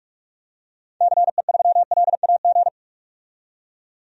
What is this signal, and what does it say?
Morse code, heard as a keyed tone, about 47 wpm, 700 Hz. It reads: CE3LAG
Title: DXCC-100 Call Signs - 6 Characters - 45wpm